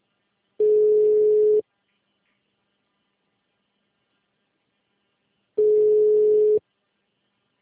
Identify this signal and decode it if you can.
Ringback tone (Europe, 425 Hz).